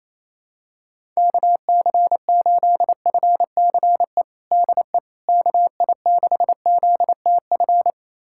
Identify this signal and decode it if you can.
Morse code, heard as a keyed tone, about 28 wpm, 700 Hz. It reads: KC8FCE DE KI6ZTF
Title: USA Callsign DE Callsign - 6 Characters - 28wpm